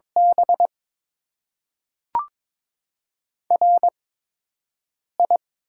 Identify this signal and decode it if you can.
Morse code, heard as a keyed tone, about 22 wpm, spaced as for 6 wpm, 700 Hz. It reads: B RI
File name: Top 20 QSO Elements - 8x Character Spacing - 22wpm